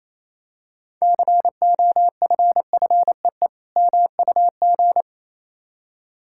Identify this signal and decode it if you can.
Morse code, heard as a keyed tone, about 27 wpm, 700 Hz. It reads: COFFEE MUG
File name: Easy Words - Speed-Racing with Hints - 22wpm